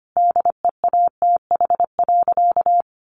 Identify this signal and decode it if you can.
Morse code, heard as a keyed tone, about 25 wpm, 700 Hz. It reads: DEATH.
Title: Sentences from Top 1100 Words 25wpm